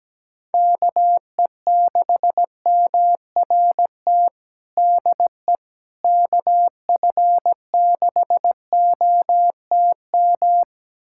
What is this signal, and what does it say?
Morse code, heard as a keyed tone, about 17 wpm, 700 Hz. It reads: KE6MRT DE KF6OTM
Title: USA Callsign DE Callsign - 6 Characters - 17wpm